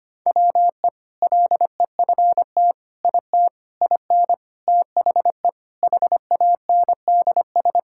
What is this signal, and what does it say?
Morse code, heard as a keyed tone, about 25 wpm, 700 Hz. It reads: WE LEFT IT IN THE HANDS
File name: practice / 25wpm / Sentences with 4-Word Phrases - 25wpm